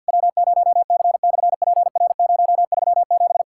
Morse code, heard as a keyed tone, about 48 wpm, 700 Hz. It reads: W0C/PR037
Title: SOTA QSO Practice 50wpm